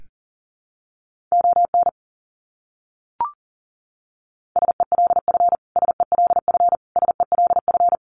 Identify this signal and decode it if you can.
Morse code, heard as a keyed tone, about 37 wpm, 700 Hz. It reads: ON SELF SELF SELF
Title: Top 300 Words Repeated 40wpm